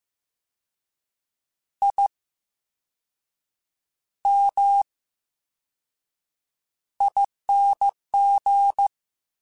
Morse code, heard as a keyed tone, about 15 wpm, spaced as for 6 wpm, 800 Hz. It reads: IMING
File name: Special INGs / Special INGs-mp3s / Slow ing @13